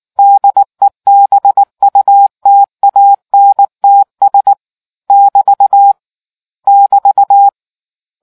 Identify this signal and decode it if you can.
Morse code, heard as a keyed tone, about 19 wpm, 800 Hz. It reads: DEBUTANTS <BT> <BT>